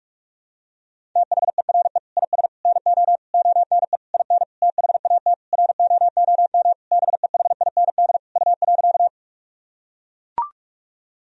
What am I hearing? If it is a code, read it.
Morse code, heard as a keyed tone, about 41 wpm, 700 Hz. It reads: THERE IS NO ONE IN THAT ROOM BEHIND U.
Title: Sentences from Top 500 Words and Abbreviations 45wpm